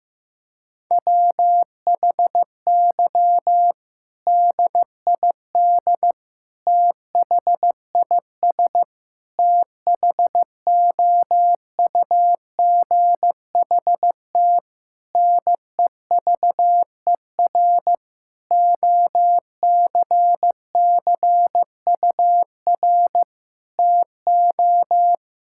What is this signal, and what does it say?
Morse code, heard as a keyed tone, about 15 wpm, 700 Hz. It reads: WHY DID THIS THOUGHT NEVER OCCUR TO
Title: Sentences from Top 1000 Words - Review Part 6 - 15wpm